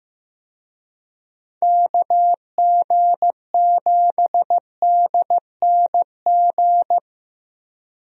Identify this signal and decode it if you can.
Morse code, heard as a keyed tone, about 15 wpm, 700 Hz. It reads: KG7DNG
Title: USA Call Signs - 6 Characters - Repeated 2x - 15wpm